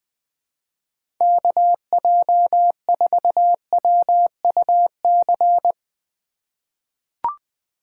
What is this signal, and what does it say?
Morse code, heard as a keyed tone, about 20 wpm, 700 Hz. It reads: KJ4WUC
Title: USA Call Signs - 6 Characters - 8x Character Spacing - 20wpm